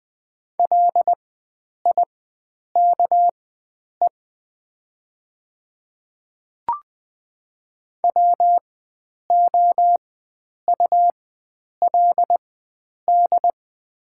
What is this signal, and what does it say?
Morse code, heard as a keyed tone, about 20 wpm, spaced as for 9 wpm, 700 Hz. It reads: LIKE WOULD
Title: 4x Character Spacing - Top 100 Words - 20wpm